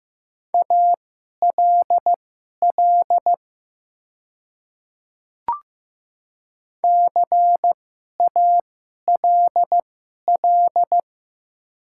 Morse code, heard as a keyed tone, about 15 wpm, spaced as for 11 wpm, 700 Hz. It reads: ALL CALL
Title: 2x Character Spacing - Top 100 Words - 15wpm